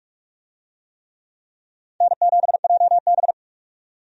Morse code, heard as a keyed tone, about 47 wpm, 700 Hz. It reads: N7JB
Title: USA Call Signs - 4 Characters - 45wpm